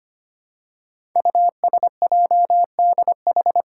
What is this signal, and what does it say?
Morse code, heard as a keyed tone, about 25 wpm, 700 Hz. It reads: USJDH
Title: Code Groups - 5 Letters - 25wpm